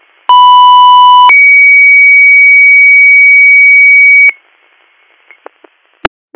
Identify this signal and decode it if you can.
Two-tone page, heard as a one-second tone, then a three-second tone, about 990 then 2250 Hz. It is heard from 0.3 to 4.3 s.